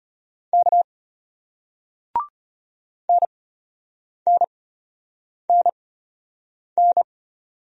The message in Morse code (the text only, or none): K NNNN